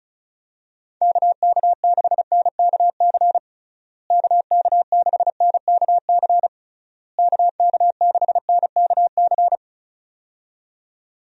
Morse code, heard as a keyed tone, about 36 wpm, 700 Hz. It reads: KK6NKC KK6NKC KK6NKC
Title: USA Call Signs - 6 Characters - Repeated 3x - 35wpm